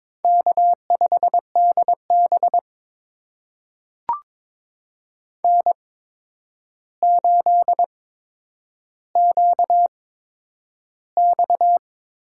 Morse code, heard as a keyed tone, about 22 wpm, spaced as for 9 wpm, 700 Hz. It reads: K5DB N8QX
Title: USA Call Signs - 4 Characters - 8x Character Spacing - 22wpm